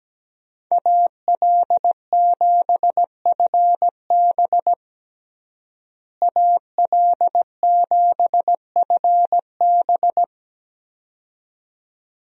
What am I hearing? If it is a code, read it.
Morse code, heard as a keyed tone, about 17 wpm, 700 Hz. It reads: AL7FB AL7FB
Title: USA Call Signs - 5 Characters - Repeated 2x - 17wpm